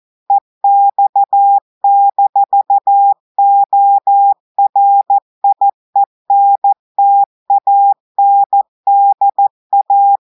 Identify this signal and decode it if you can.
Morse code, heard as a keyed tone, about 14 wpm, 800 Hz. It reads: EX-ORIENTANDA
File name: audios / portugais / portugais texte en clair 1 vitesse 14 mots minutes